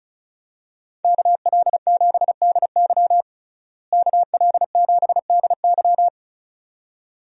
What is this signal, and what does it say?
Morse code, heard as a keyed tone, about 36 wpm, 700 Hz. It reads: KL7DY KL7DY
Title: DXCC-100 Call Signs - 5 Characters - Repeated 2x - 35wpm